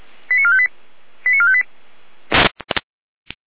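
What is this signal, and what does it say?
Five-tone selective call (ZVEI), heard as quick runs of five tones, tones of about 70 ms. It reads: 89458 89458
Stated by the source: ZVEI five-tone selective call received on SDR, feuerwehr-orlamuende.org fire brigade archive.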